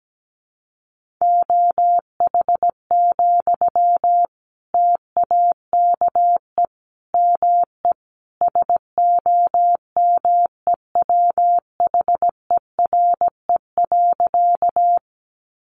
Morse code, heard as a keyed tone, about 17 wpm, 700 Hz. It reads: OH, TAKE ME SOMEWHERE.
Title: Sentences from Top 1400 Words 17wpm